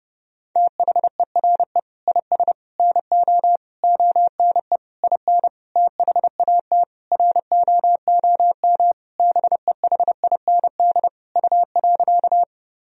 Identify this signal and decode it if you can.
Morse code, heard as a keyed tone, about 28 wpm, 700 Hz. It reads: THERE IS NO ONE IN THAT ROOM BEHIND U.
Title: Sentences from Top 500 Words and Abbreviations 30wpm